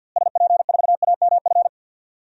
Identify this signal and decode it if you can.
Morse code, heard as a keyed tone, about 53 wpm, 700 Hz. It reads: SP4AMF
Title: DXCC-100 Call Signs - 6 Characters - Repeated 2x - 50wpm